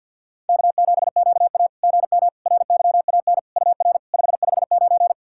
Morse code, heard as a keyed tone, about 53 wpm, 690 Hz. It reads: K7QA GM RYAN UR 559
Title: SOTA QSO Practice 50wpm